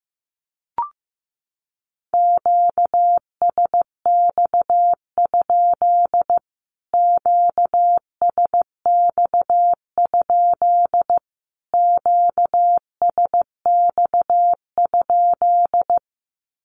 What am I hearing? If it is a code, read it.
Morse code, heard as a keyed tone, about 15 wpm, 700 Hz. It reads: QSX? QSX? QSX?
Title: Q-Codes Repeated 3x 15wpm